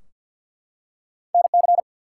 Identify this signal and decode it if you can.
Morse code, heard as a keyed tone, about 50 wpm, 700 Hz. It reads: NC
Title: US States and Canadian Provinces 50wpm